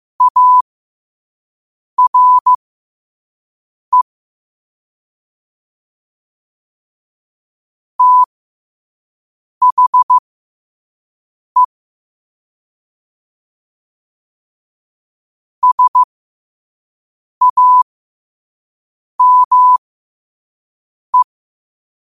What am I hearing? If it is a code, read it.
Morse code, heard as a keyed tone, about 15 wpm, spaced as for 5 wpm, 1000 Hz. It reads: ARE THE SAME
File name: qotdfiles / 5wpm